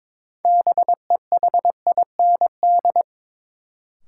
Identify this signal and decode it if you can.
Morse code, heard as a keyed tone, about 22 wpm, 700 Hz. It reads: BEHIND
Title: Sentences from Top 500 Words - New Word Drill - 22wpm